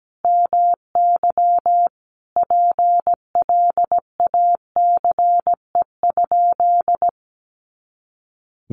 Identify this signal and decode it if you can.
Morse code, heard as a keyed tone, about 17 wpm, 700 Hz. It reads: MY PLACE?
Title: Sentences from Top 200 Words 17wpm